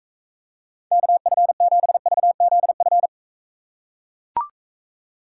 Morse code, heard as a keyed tone, about 41 wpm, 700 Hz. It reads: KF7VZF